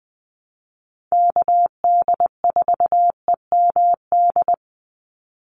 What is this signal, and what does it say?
Morse code, heard as a keyed tone, about 20 wpm, 700 Hz. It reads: KD4EMD
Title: USA Call Signs 20wpm